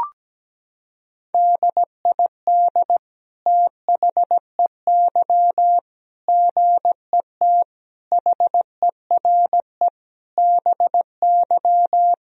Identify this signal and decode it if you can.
Morse code, heard as a keyed tone, about 17 wpm, 700 Hz. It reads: DID THEY GET HERE BY
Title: Sentences from Top 1000 Words - Review Part 3 - 17wpm